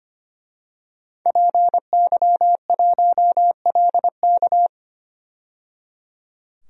Morse code, heard as a keyed tone, about 25 wpm, 700 Hz. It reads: PY1LK
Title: DXCC-100 Call Signs - 5 Characters - 25wpm